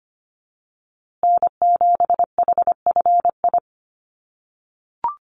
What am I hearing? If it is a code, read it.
Morse code, heard as a keyed tone, about 25 wpm, 700 Hz. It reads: N7HFI
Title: USA Call Signs Repeated 2x 25wpm